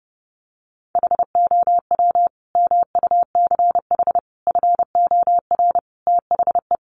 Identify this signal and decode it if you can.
Morse code, heard as a keyed tone, about 30 wpm, 700 Hz. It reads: HOW MUCH FOR THE
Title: Sentences from Top 1200 Words 30wpm